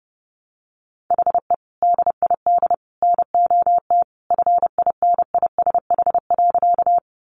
Morse code, heard as a keyed tone, about 28 wpm, 700 Hz. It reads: HE DID NOT FINISH.
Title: Sentences from Top 1200 Words 30wpm